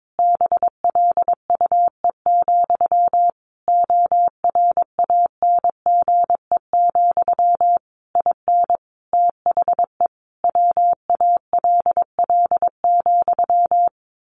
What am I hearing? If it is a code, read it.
Morse code, heard as a keyed tone, about 22 wpm, 700 Hz. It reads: BLUE, ORANGE, IN THE WALL,
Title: A Sound Of Thunder 22wpm